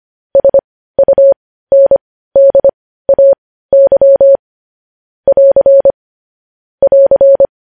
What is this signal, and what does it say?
Morse code, heard as a keyed tone, about 25 wpm, spaced as for 15 wpm, 550 Hz. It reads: SUNDAY <AR> <AR>
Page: News Headlines in Morse Code at 15 WPM